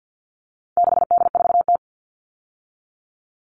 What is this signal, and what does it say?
Morse code, heard as a keyed tone, about 50 wpm, 700 Hz. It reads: 6D4T
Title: All Cut Numbers - 4-Digit 50wpm